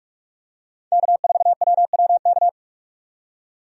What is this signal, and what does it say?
Morse code, heard as a keyed tone, about 46 wpm, 700 Hz. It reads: K4WWK
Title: USA Call Signs - 5 Characters - 45wpm